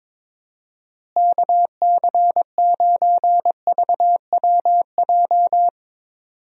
Morse code, heard as a keyed tone, about 22 wpm, 700 Hz. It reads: KC9VWJ
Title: USA Call Signs - 6 Characters - Repeated 3x - 22wpm